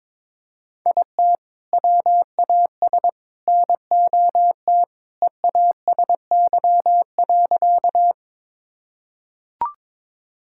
Morse code, heard as a keyed tone, about 22 wpm, 700 Hz. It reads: IT WAS NOT EASY.
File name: practice / 22wpm / Sentences with 3-Word Phrases - 22wpm